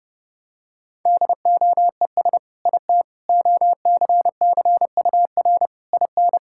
Morse code, heard as a keyed tone, about 30 wpm, 700 Hz. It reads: DOES IT OCCUR IN
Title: Sentences from Top 1000 Words - Review Part 6 - 30wpm